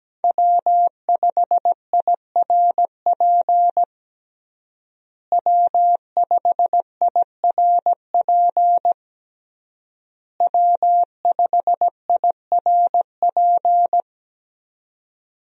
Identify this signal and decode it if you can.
Morse code, heard as a keyed tone, about 17 wpm, 700 Hz. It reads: W5IRP W5IRP W5IRP
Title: USA Call Signs - 5 Characters - Repeated 3x - 17wpm